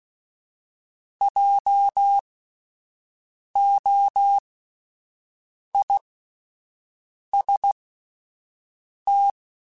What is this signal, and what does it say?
Morse code, heard as a keyed tone, about 16 wpm, spaced as for 5 wpm, 800 Hz. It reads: JOIST
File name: Spaced Alphas / Spaced Alphas mp3s / Js sp @10